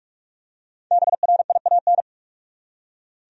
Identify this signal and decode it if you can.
Morse code, heard as a keyed tone, about 44 wpm, 700 Hz. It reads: BRIAN
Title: USA Common Names - Max 5 Letters - 45wpm